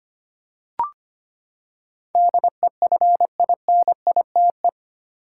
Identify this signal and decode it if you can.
Morse code, heard as a keyed tone, about 25 wpm, 700 Hz. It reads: DEFINITE
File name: Eight Letter Words - 25wpm